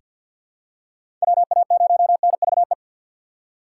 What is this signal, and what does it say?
Morse code, heard as a keyed tone, about 47 wpm, 700 Hz. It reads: WA0NVE
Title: USA Call Signs - 3x Character Spacing - 50wpm